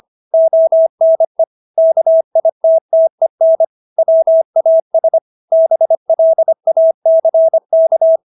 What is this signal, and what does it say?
Morse code, heard as a keyed tone, about 25 wpm, 650 Hz. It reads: ONE KITTEN WAS BLACK